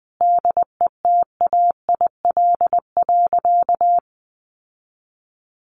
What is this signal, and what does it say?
Morse code, heard as a keyed tone, about 20 wpm, 700 Hz. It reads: DETAIL.
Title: Sentences from Top 1500 Words 20wpm